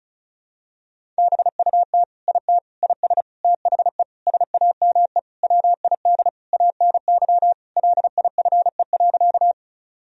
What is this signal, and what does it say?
Morse code, heard as a keyed tone, about 34 wpm, 700 Hz. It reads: BUT IT IS THE SAME WID ANY LIFE.
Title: Sentences from Top 300 Words and Abbreviations 35wpm